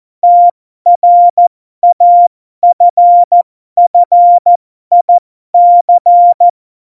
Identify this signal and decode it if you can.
Morse code, heard as a keyed tone, about 14 wpm, spaced as for 12 wpm, 700 Hz. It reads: TRAFFIC
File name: Morsekurs / Witze